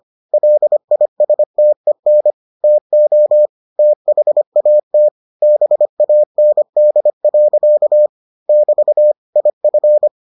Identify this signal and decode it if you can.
Morse code, heard as a keyed tone, about 25 wpm, 600 Hz. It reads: LISTEN TO THAT BAND. <BT> IF